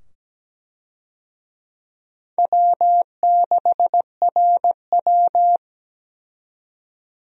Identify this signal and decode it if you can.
Morse code, heard as a keyed tone, about 17 wpm, 700 Hz. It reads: W6RW
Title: USA Call Signs - 4 Characters - 17wpm